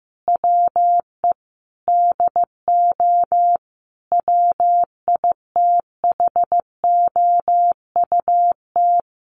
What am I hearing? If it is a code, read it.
Morse code, heard as a keyed tone, about 15 wpm, 700 Hz. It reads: WE DO WITHOUT
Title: Sentences from Top 600 Words 15wpm